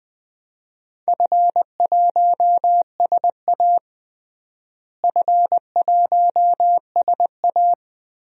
Morse code, heard as a keyed tone, about 20 wpm, 700 Hz. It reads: F1SA F1SA
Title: DXCC-100 Call Signs - 4 Characters - Repeated 3x - 20wpm